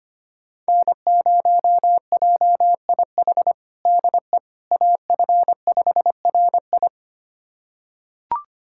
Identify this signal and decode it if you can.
Morse code, heard as a keyed tone, about 25 wpm, 700 Hz. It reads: N0JIH DE AF5RI
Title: USA Callsign DE Callsign - 5 Characters - 25wpm